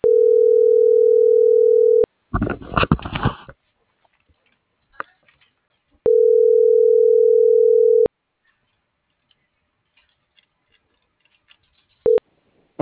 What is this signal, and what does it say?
Ringback tone (North America), 440 + 480 Hz.
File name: CENTRAL SOTOVA / out alejandro.rodriguez - .51448